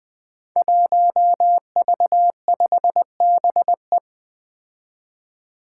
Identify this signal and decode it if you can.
Morse code, heard as a keyed tone, about 20 wpm, 700 Hz. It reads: 1V5BE
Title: Five Challenging Letter-Number Combinations 20wpm